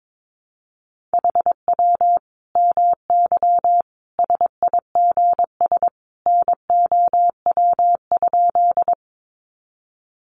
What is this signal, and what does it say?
Morse code, heard as a keyed tone, about 22 wpm, 700 Hz. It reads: HW MY SIGS NOW?
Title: Common QSO Phrases 22wpm